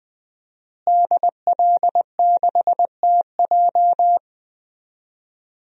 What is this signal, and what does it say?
Morse code, heard as a keyed tone, about 20 wpm, 700 Hz. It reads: DL6TJ
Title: DXCC-100 Call Signs - 5 Characters - Repeated 3x - 20wpm